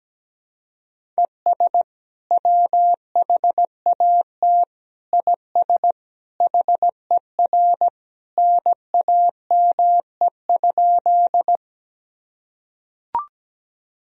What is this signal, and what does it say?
Morse code, heard as a keyed tone, about 17 wpm, 700 Hz. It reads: ES WHAT IS HER NAME?
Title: Sentences from Top 200 Words and Abbreviations 17wpm